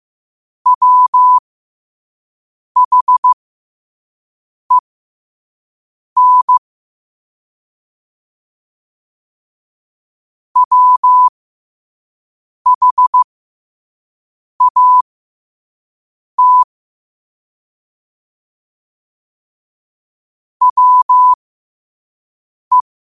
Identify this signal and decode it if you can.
Morse code, heard as a keyed tone, about 15 wpm, spaced as for 5 wpm, 1000 Hz. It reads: WHEN WHAT WE
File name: qotdfiles / 5wpm